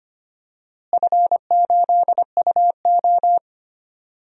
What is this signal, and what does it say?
Morse code, heard as a keyed tone, about 25 wpm, 700 Hz. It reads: F8UO
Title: DXCC-100 Call Signs - 4 Characters - Repeated 2x - 25wpm